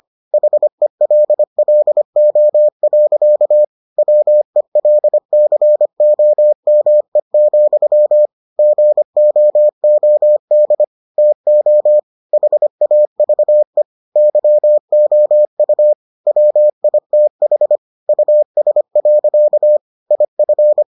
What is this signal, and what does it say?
Morse code, heard as a keyed tone, about 25 wpm, 600 Hz. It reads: HELLO. WELCOME, GOOD TO HAVE YOU WITH US. IF